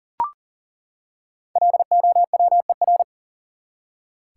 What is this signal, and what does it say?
Morse code, heard as a keyed tone, about 40 wpm, 700 Hz. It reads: LOWER
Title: Five Letter Words - 40wpm